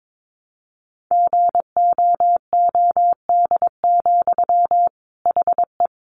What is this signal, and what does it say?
Morse code, heard as a keyed tone, about 22 wpm, 700 Hz. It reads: GOOD, HE
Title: Sentences from Top 200 Words 22wpm